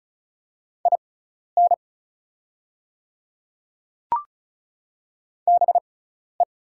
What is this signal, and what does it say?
Morse code, heard as a keyed tone, about 34 wpm, spaced as for 12 wpm, 690 Hz. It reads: IN BE
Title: 6x Character Spacing - Top 500 Words - 35wpm